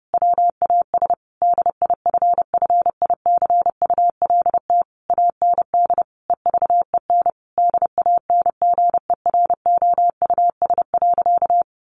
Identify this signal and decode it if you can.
Morse code, heard as a keyed tone, about 30 wpm, 700 Hz. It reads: WAS DIFFICULT AND EVEN DANGEROUS.
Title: Sentences from Top 1200 Words 30wpm